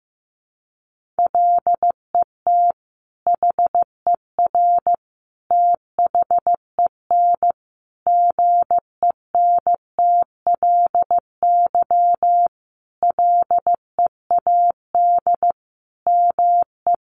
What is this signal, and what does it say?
Morse code, heard as a keyed tone, about 15 wpm, 700 Hz. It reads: LET HER THEN GENTLY LEAD ME
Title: Sentences from Top 1500 Words 15wpm